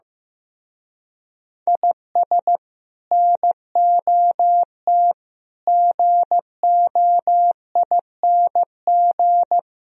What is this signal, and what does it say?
Morse code, heard as a keyed tone, about 17 wpm, 700 Hz. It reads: IS NOT GOING
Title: Speed-Racing - Sets of 4 Words 15wpm